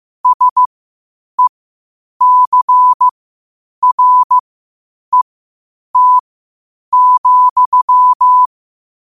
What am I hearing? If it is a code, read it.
Morse code, heard as a keyed tone, about 15 wpm, spaced as for 8 wpm, 1000 Hz. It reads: SECRET,
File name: qotdfiles / 8wpm